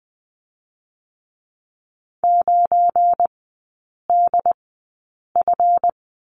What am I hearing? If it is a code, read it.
Morse code, heard as a keyed tone, about 20 wpm, spaced as for 8 wpm, 700 Hz. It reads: 9DF
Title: 172 Lesson 28e - Number 9 - Three Characters 20wpm